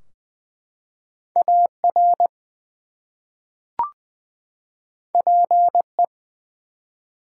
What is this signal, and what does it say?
Morse code, heard as a keyed tone, about 20 wpm, 700 Hz. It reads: AR PE
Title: US States and Canadian Provinces 20wpm